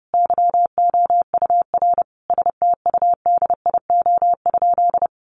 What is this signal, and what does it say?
Morse code, heard as a keyed tone, about 30 wpm, 700 Hz. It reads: YOUR STUDIO?
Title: Sentences from Top 1200 Words 30wpm